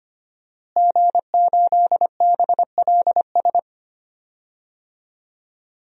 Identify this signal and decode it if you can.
Morse code, heard as a keyed tone, about 25 wpm, 700 Hz. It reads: G8BLS
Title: DXCC-100 Call Signs - 5 Characters - Repeated 2x - 25wpm